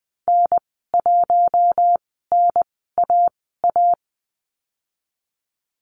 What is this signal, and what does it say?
Morse code, heard as a keyed tone, about 20 wpm, spaced as for 14 wpm, 700 Hz. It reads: N1NAA
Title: 076 Lesson 13g - Letter L - Call Signs 20wpm